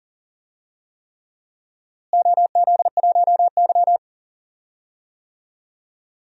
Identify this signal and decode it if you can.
Morse code, heard as a keyed tone, about 40 wpm, 700 Hz. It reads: OZ1Y
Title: DXCC-100 Call Signs - 4 Characters - Repeated 3x - 40wpm